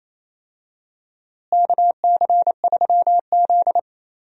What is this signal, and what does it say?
Morse code, heard as a keyed tone, about 28 wpm, 700 Hz. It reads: KC3Z
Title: USA Call Signs - 4 Characters - Repeated 3x - 28wpm